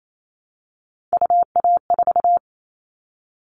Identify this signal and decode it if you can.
Morse code, heard as a keyed tone, about 28 wpm, 700 Hz. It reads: UA4
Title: Callsign Prefixes for Top 25 DXCC - Repeated 3x - 28wpm